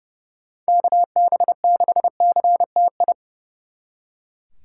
Morse code, heard as a keyed tone, about 40 wpm, 700 Hz. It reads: KB6CTI